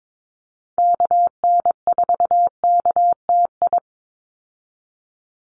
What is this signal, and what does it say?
Morse code, heard as a keyed tone, about 22 wpm, 700 Hz. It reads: KN4KTI
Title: USA Call Signs 22wpm